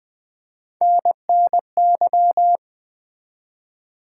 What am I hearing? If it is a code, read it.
Morse code, heard as a keyed tone, about 20 wpm, 700 Hz. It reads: NNY
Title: ARRL Sections v2 - 20wpm